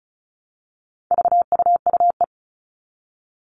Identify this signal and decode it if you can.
Morse code, heard as a keyed tone, about 34 wpm, 700 Hz. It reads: VUUE
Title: All Cut Numbers - 4-Digit 35wpm